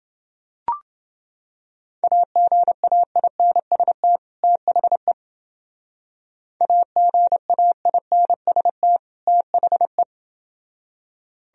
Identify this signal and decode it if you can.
Morse code, heard as a keyed tone, about 28 wpm, 700 Hz. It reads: AGAINST THE AGAINST THE